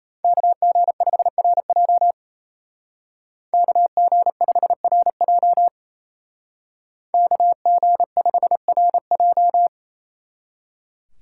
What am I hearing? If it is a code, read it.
Morse code, heard as a keyed tone, about 32 wpm, 700 Hz. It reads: KG5RJ KG5RJ KG5RJ
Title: Speed Racing - USA 2x2 Call Signs 28wpm